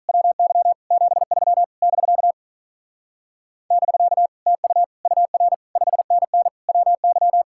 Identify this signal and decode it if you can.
Morse code, heard as a keyed tone, about 44 wpm, 700 Hz. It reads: WY 73 <BK> <BK> TU UR 5NN WY